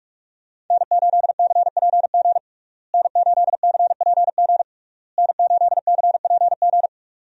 Morse code, heard as a keyed tone, about 46 wpm, 700 Hz. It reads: N8CPG N8CPG N8CPG
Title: USA Call Signs - 5 Characters - Repeated 3x - 45wpm